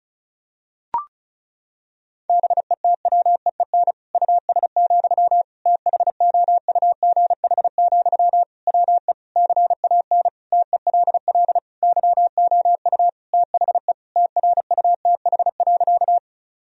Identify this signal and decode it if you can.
Morse code, heard as a keyed tone, about 34 wpm, 700 Hz. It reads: BETWEEN US, THOUGH, WE CAN TELL YOU THE TRUTH.
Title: Sentences from Top 1000 Words - Review Part 2 - 35wpm